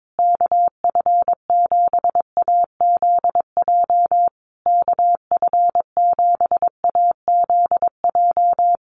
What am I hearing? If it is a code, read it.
Morse code, heard as a keyed tone, about 22 wpm, 700 Hz. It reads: KF7AZJ KF7AZJ
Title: USA Call Signs Repeated 3x 22wpm